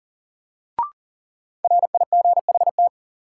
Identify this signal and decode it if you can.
Morse code, heard as a keyed tone, about 38 wpm, 700 Hz. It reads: RIGHT
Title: Sentences from Top 200 Words - New Word Drill - 40wpm